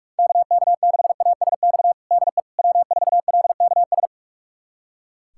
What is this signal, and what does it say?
Morse code, heard as a keyed tone, about 46 wpm, 700 Hz. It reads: KK6ASX DE W4LKS
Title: USA Callsign DE Callsign 45wpm